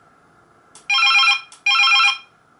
There is no code.